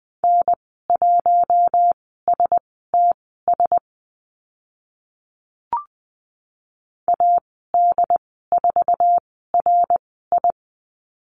Morse code, heard as a keyed tone, about 20 wpm, spaced as for 14 wpm, 700 Hz. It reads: N1STS AD4RI
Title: 082 Lesson 14g - Number 2 - Call Signs 20wpm